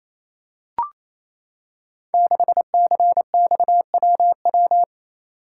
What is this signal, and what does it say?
Morse code, heard as a keyed tone, about 27 wpm, 690 Hz. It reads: 6CXWW
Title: Code Groups - 5 Letter:Numbers - 28wpm